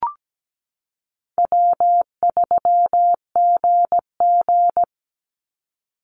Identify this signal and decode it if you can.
Morse code, heard as a keyed tone, about 17 wpm, 700 Hz. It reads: W3GG
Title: Easy Call Signs - 17wpm